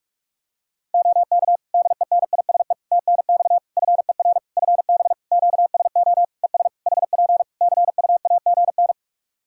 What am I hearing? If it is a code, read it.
Morse code, heard as a keyed tone, about 47 wpm, 700 Hz. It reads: OK DENISE TNX FER FB QSO ES HP CUAGN
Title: K1USN SST QSO Practice 45wpm